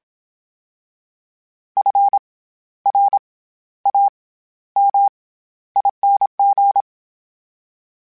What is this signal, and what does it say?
Morse code, heard as a keyed tone, about 27 wpm, 800 Hz. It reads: F R A M ING
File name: Special INGs-mp3s / Special INGs-mp3s / SlowToFaster ing @25